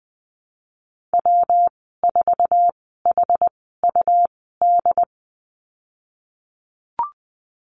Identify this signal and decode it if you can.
Morse code, heard as a keyed tone, about 20 wpm, spaced as for 15 wpm, 700 Hz. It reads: W4HUD
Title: 200 Lesson 32g - Letter k - Call Signs 20wpm